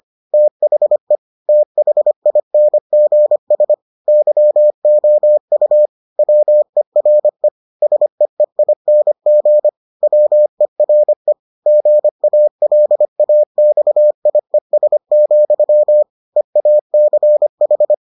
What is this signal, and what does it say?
Morse code, heard as a keyed tone, about 25 wpm, 600 Hz. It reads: THE THINGS YOU WERE SEEING WERE GALAXIES, EACH